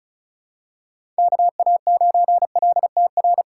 Morse code, heard as a keyed tone, about 34 wpm, 700 Hz. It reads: KA9LTR
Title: USA Call Signs - 6 Characters - 2x Character Spacing - 35wpm